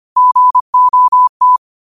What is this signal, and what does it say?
Morse code, heard as a keyed tone, about 25 wpm, 1000 Hz. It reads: GOT